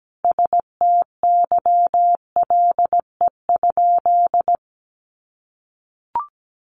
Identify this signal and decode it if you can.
Morse code, heard as a keyed tone, about 17 wpm, 700 Hz. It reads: STYLE?
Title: Sentences from Top 800 Words 17wpm